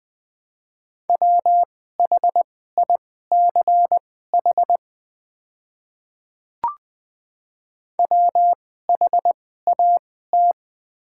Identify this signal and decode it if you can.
Morse code, heard as a keyed tone, about 20 wpm, spaced as for 14 wpm, 700 Hz. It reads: WHICH WHAT
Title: 2x Character Spacing - Top 100 Words - 20wpm